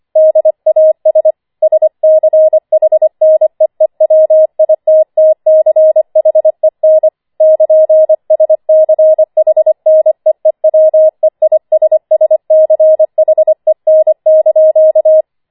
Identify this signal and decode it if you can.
Morse code, heard as a keyed tone, about 25 wpm, 630 Hz. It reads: DAS SCHNEEWITTCHEN <KN>SCHNEEWEISSCHEN)
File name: 25wpm